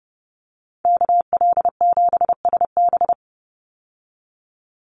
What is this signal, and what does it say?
Morse code, heard as a keyed tone, about 30 wpm, 700 Hz. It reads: KL7SB
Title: Call Signs Repeated 2x - 30wpm